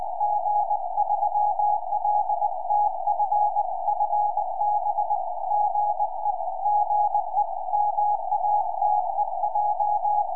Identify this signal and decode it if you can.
Morse code, heard as a keyed tone, about 19 wpm, 830 Hz. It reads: G3LTF FB GE GE MATEJ
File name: tropo / g3ltf 70cm